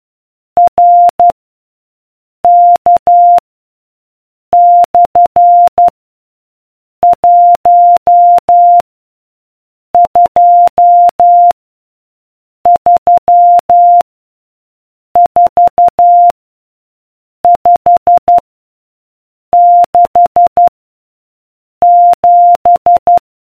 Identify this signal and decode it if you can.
Morse code, heard as a keyed tone, about 12 wpm, 700 Hz. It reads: R K / 1 2 3 4 5 6 7